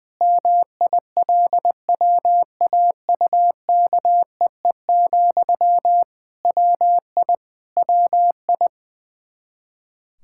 Morse code, heard as a keyed tone, about 21 wpm, 700 Hz. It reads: MILWAUKEE, WI WI
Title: Practice QSO - Exchange 1 - Beginning - 20wpm